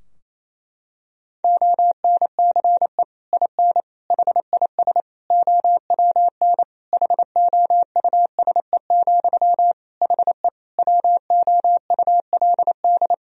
Morse code, heard as a keyed tone, about 28 wpm, 700 Hz. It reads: ONCE IN HIS OWN HOUSE, HE WOULD
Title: Sentences from Top 1000 Words - Review Part 2 - 28wpm